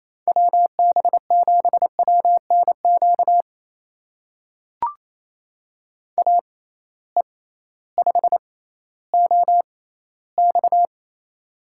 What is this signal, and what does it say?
Morse code, heard as a keyed tone, about 28 wpm, spaced as for 15 wpm, 700 Hz. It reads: WB7WNQ AE5OX
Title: USA Call Signs - 6x Character Spacing - 28wpm